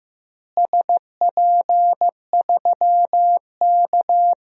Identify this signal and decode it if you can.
Morse code, heard as a keyed tone, about 15 wpm, 700 Hz. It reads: SP3K
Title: DXCC-100 Call Signs - 4 Characters - Repeated 3x - 15wpm